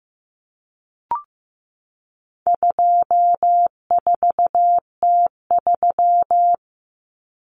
Morse code, heard as a keyed tone, about 15 wpm, 700 Hz. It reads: 24T3
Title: Common Cut Numbers - 4-Digit 15wpm